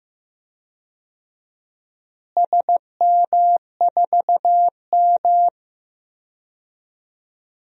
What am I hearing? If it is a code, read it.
Morse code, heard as a keyed tone, about 15 wpm, 700 Hz. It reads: SM4M
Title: DXCC-100 Call Signs - 4 Characters - Repeated 3x - 15wpm